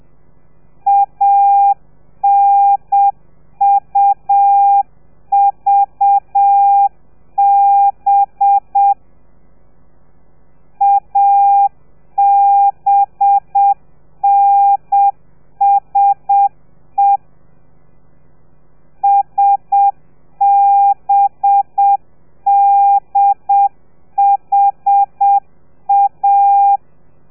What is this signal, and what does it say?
Morse code, heard as a keyed tone, about 7 wpm, 800 Hz. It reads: ANUVB ABNSE SBDHA
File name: telegrafia / i2tfj / Lez4 30